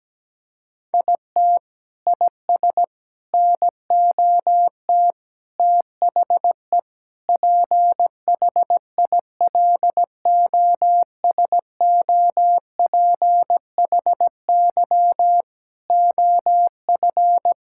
Morse code, heard as a keyed tone, about 17 wpm, 700 Hz. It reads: IT IS NOT THE PHILOSOPHY OF
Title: Sentences from Top 1000 Words - Review Part 6 - 17wpm